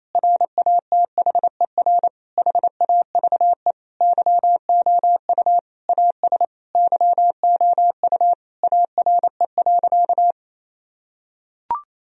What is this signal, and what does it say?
Morse code, heard as a keyed tone, about 28 wpm, 700 Hz. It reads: RATHER HAVE YOU AS YOU ARE.
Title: Sentences from Top 1000 Words - Review Part 4 - 28wpm